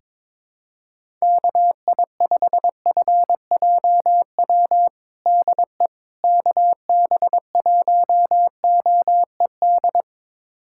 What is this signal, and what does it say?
Morse code, heard as a keyed tone, about 22 wpm, 700 Hz. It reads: KI5FJW DE KB1OED
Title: USA Callsign DE Callsign - 6 Characters - 22wpm